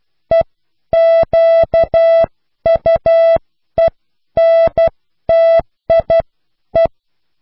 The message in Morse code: EQUENTIE